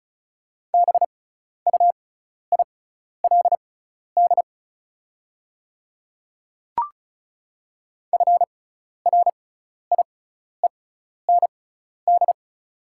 Morse code, heard as a keyed tone, about 36 wpm, spaced as for 12 wpm, 690 Hz. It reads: BUILD FRIEND
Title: 6x Character Spacing - Top 300 Words - 35wpm